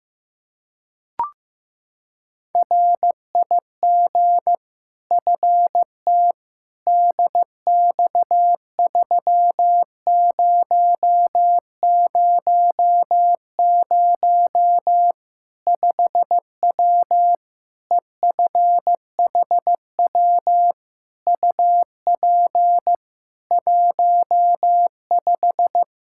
Morse code, heard as a keyed tone, about 15 wpm, 700 Hz. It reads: RIG FT DX3000 5W EFHW UP 15
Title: K1USN SST QSO Practice 15wpm